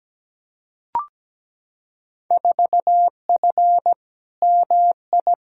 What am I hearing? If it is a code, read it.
Morse code, heard as a keyed tone, about 17 wpm, 700 Hz. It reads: 4F MI
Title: Field Day Exchanges - 17wpm